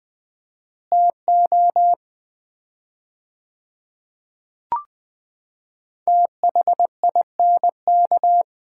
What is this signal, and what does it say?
Morse code, heard as a keyed tone, about 20 wpm, 700 Hz. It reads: TO THINK
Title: 4x Word-Spacing - Sets of 5 Words - 20wpm